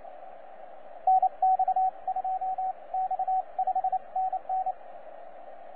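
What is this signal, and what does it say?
Morse code, heard as a keyed tone, about 28 wpm, spaced as for 25 wpm, 750 Hz, with radio noise. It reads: NX2X 5NN
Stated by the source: TZ6BB on 12 metres CW, from DX World.net.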